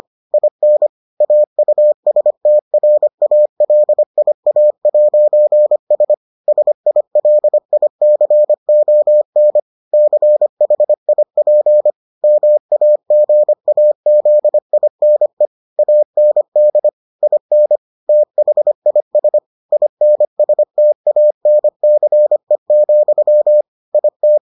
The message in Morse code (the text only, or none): IN AUSTRALIA'S SILICON CHIP MAGAZINE AND IN THIS INSTANCE, IT